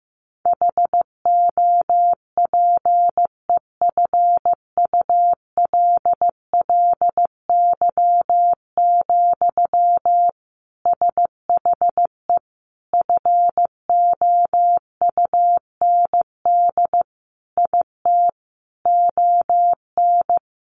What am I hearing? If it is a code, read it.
Morse code, heard as a keyed tone, about 15 wpm, 700 Hz. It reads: HOPEFULLY, SHE FOUND IT ON
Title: Sentences from Top 1500 Words 15wpm